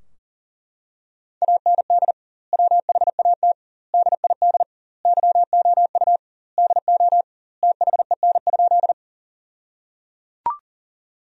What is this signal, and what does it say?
Morse code, heard as a keyed tone, about 37 wpm, 700 Hz. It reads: AND WHAT DID YOU DO THEN?
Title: 2x Word-Spacing - Sentences from Top 100 Words - 40wpm